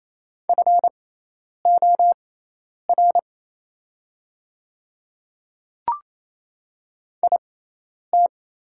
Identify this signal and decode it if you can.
Morse code, heard as a keyed tone, about 27 wpm, spaced as for 10 wpm, 700 Hz. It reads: FOR IT
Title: 6x Character Spacing - Top 100 Words - 28wpm